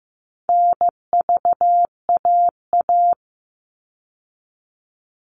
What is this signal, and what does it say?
Morse code, heard as a keyed tone, about 15 wpm, 700 Hz. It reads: NVAA
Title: All Cut Numbers - 4-Digit 15wpm